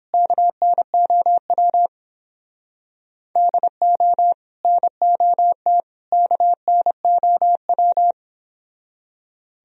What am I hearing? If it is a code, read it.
Morse code, heard as a keyed tone, about 31 wpm, 700 Hz. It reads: KNOW DO NOT KNOW